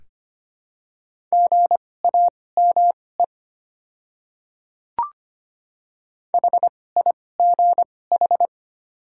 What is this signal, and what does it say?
Morse code, heard as a keyed tone, about 25 wpm, spaced as for 18 wpm, 700 Hz. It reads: GAME HIGH
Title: 2x Character Spacing - Top 500 Words - 25wpm